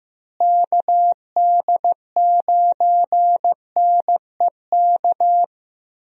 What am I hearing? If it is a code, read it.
Morse code, heard as a keyed tone, about 15 wpm, 700 Hz. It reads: KD9NEK
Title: USA Call Signs - 6 Characters - Repeated 3x - 15wpm